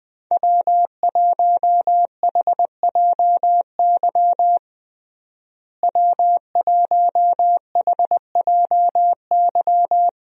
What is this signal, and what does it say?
Morse code, heard as a keyed tone, about 20 wpm, 700 Hz. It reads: W1HJY W1HJY
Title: USA Call Signs - 5 Characters - Repeated 2x - 20wpm